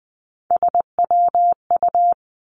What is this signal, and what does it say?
Morse code, heard as a keyed tone, about 20 wpm, 700 Hz. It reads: SWU
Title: Three Challenging Letter-Number Combinations 20wpm